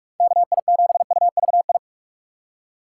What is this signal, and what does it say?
Morse code, heard as a keyed tone, about 56 wpm, 700 Hz. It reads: KI7UVI
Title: Speed Racing - USA Call Signs 45wpm